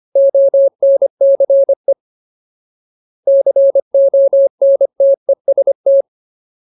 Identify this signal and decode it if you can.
Morse code, heard as a keyed tone, about 25 wpm, 550 Hz. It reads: ONCE CONTEST
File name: Ultimate Word List / 25WPM Ultimate